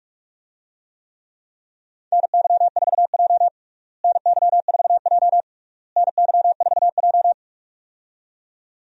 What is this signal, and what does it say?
Morse code, heard as a keyed tone, about 45 wpm, 700 Hz. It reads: NY4J NY4J NY4J
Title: USA Call Signs - 4 Characters - Repeated 3x - 45wpm